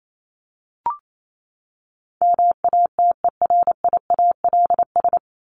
Morse code, heard as a keyed tone, about 28 wpm, 700 Hz. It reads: MATERIALS
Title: ES:S Suffix - Top 100 Words with Base Words 28wpm